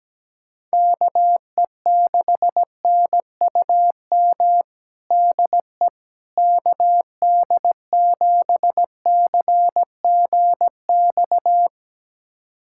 Morse code, heard as a keyed tone, about 17 wpm, 700 Hz. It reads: KE6NUM DE KD7CGX
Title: USA Callsign DE Callsign - 6 Characters - 17wpm